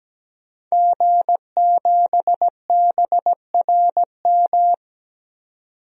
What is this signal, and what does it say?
Morse code, heard as a keyed tone, about 17 wpm, 700 Hz. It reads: G7BRM
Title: DXCC-100 Call Signs - 5 Characters - 17wpm